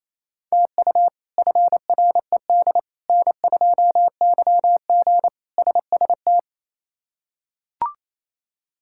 Morse code, heard as a keyed tone, about 28 wpm, 700 Hz. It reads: TU FRED N2YG SST
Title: K1USN SST QSO Practice - 28wpm